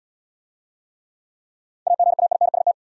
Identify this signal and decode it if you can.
Morse code, heard as a keyed tone, about 62 wpm, 700 Hz. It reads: AGAINST
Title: Speed-Racing Top 500 Words 50wpm